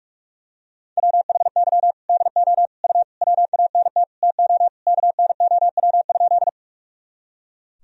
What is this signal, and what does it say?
Morse code, heard as a keyed tone, about 41 wpm, 700 Hz. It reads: WHY DO U WANT TO KNOW?
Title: Sentences from Top 200 Words and Abbreviations 45wpm